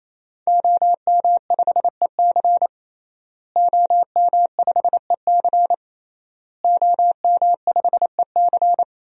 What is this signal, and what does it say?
Morse code, heard as a keyed tone, about 27 wpm, 700 Hz. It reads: OM5EC OM5EC OM5EC
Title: DXCC-100 Call Signs - 5 Characters - Repeated 3x - 28wpm